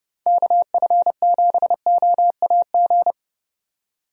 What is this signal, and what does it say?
Morse code, heard as a keyed tone, about 32 wpm, 700 Hz. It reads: KF7OAG
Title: USA Call Signs - 6 Characters - 30wpm